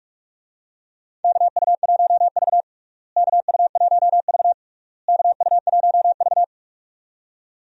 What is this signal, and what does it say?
Morse code, heard as a keyed tone, about 44 wpm, 700 Hz. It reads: KU1V KU1V KU1V
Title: USA Call Signs - 4 Characters - Repeated 3x - 45wpm